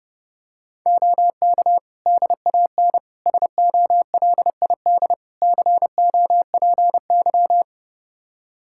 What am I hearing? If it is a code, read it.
Morse code, heard as a keyed tone, about 32 wpm, 700 Hz. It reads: OK DAN SOLID COPY
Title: Practice QSO - Exchange 3 - More Info - 30wpm